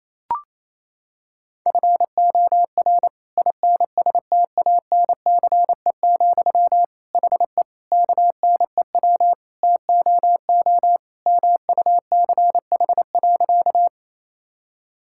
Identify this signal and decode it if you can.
Morse code, heard as a keyed tone, about 27 wpm, 700 Hz. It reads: FOR INSTANCE, HE KNEW TOO MUCH.
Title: Sentences from Top 1000 Words - Review Part 5 - 28wpm